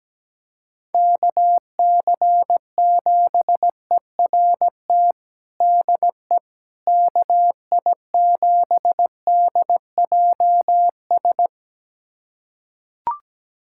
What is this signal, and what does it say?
Morse code, heard as a keyed tone, about 17 wpm, 700 Hz. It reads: KC7ERT DE KI7DJS
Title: USA Callsign DE Callsign - 6 Characters - 17wpm